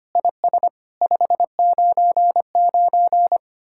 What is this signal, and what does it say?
Morse code, heard as a keyed tone, about 30 wpm, 700 Hz. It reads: IS 599
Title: Speed-Racing Common QSO Phrases - 25wpm